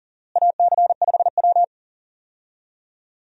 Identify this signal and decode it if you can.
Morse code, heard as a keyed tone, about 40 wpm, 700 Hz. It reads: AC5W
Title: USA Call Signs - 4 Characters - Repeated 2x - 40wpm